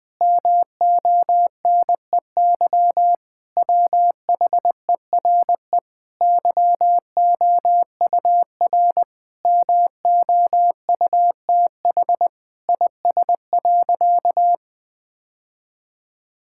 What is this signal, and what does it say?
Morse code, heard as a keyed tone, about 20 wpm, 700 Hz. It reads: MONEY WHERE YOUR MOUTH IS.
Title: Sentences with 7-Word Phrases - 20wpm